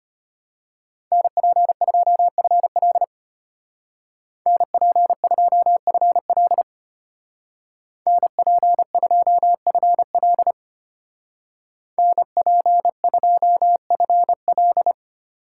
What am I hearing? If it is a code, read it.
Morse code, heard as a keyed tone, about 34 wpm, 700 Hz. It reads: NP2FL NP2FL NP2FL NP2FL